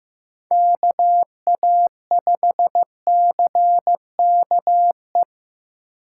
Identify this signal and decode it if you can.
Morse code, heard as a keyed tone, about 18 wpm, 700 Hz. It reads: KA5CKE
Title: Speed Racing - USA Call Signs 15wpm